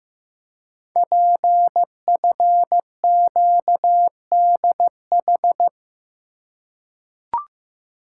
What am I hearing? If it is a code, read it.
Morse code, heard as a keyed tone, about 15 wpm, 700 Hz. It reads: PFQDH
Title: Code Groups - 5 Letters - 15wpm